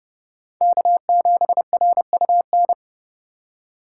Morse code, heard as a keyed tone, about 36 wpm, 700 Hz. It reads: K7RUN
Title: Speed Racing - USA 1x3 Call Signs 30wpm